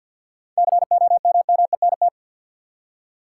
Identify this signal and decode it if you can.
Morse code, heard as a keyed tone, about 50 wpm, 700 Hz. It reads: COMMENT
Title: Seven Letter Words - 50wpm